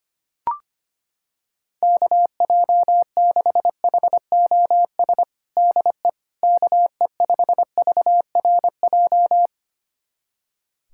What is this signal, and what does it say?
Morse code, heard as a keyed tone, about 25 wpm, 700 Hz. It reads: KJ6HOS DE KE5VRJ
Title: USA Callsign DE Callsign - 6 Characters - 25wpm